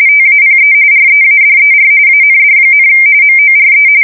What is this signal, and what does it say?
RTTY, about 45.45 baud. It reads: TAPE READER AND, MORE
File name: fsk170 / teletype-RTTY